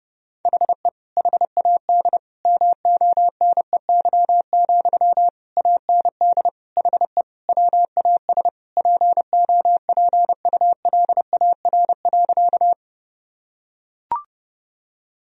Morse code, heard as a keyed tone, about 30 wpm, 700 Hz. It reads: HE HAD MONEY, AND HE WAS POPULAR.
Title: Sentences from Top 1000 Words - Review Part 3 - 30wpm